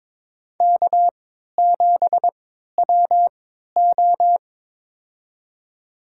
Morse code, heard as a keyed tone, about 22 wpm, spaced as for 13 wpm, 700 Hz. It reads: K7WO
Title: USA Call Signs - 3x Character Spacing - 22wpm